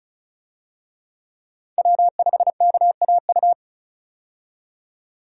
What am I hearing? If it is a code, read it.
Morse code, heard as a keyed tone, about 36 wpm, 700 Hz. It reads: W5KAU
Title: USA Call Signs - 5 Characters - 35wpm